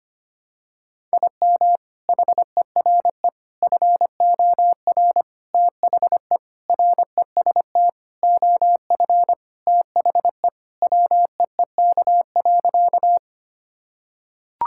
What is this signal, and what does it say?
Morse code, heard as a keyed tone, about 25 wpm, 700 Hz. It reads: IM HERE FOR THE REST OF THE WEEK.
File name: practice / 25wpm / Sentences with 4-Word Phrases - 25wpm